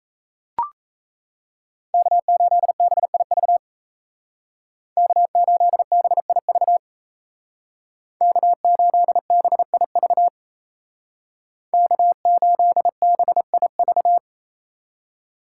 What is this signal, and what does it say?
Morse code, heard as a keyed tone, about 36 wpm, 700 Hz. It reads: K8BIV K8BIV K8BIV K8BIV